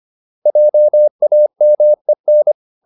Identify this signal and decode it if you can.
Morse code, heard as a keyed tone, about 25 wpm, 600 Hz. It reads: JAMEN
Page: Ferie 25 wpm